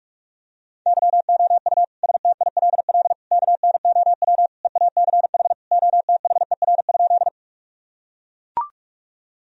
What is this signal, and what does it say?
Morse code, heard as a keyed tone, about 42 wpm, 700 Hz. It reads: YOU STILL KNOW EACH OTHER?
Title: Sentences from Top 1000 Words - Review Part 1 - 45wpm